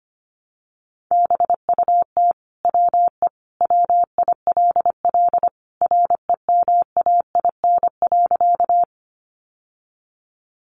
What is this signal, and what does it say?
Morse code, heard as a keyed tone, about 25 wpm, 700 Hz. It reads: BUT WE WILL REMAIN.
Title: Sentences from Top 1200 Words 25wpm